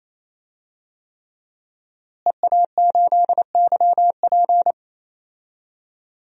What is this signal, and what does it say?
Morse code, heard as a keyed tone, about 28 wpm, 700 Hz. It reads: EA8YP
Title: DXCC-100 Call Signs - 5 Characters - Repeated 3x - 28wpm